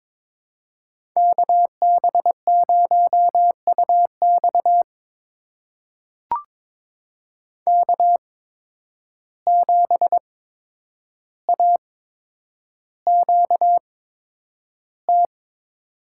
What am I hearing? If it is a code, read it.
Morse code, heard as a keyed tone, about 22 wpm, spaced as for 9 wpm, 700 Hz. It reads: KB0UX K7AQT
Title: USA Call Signs - 5 Characters - 8x Character Spacing - 22wpm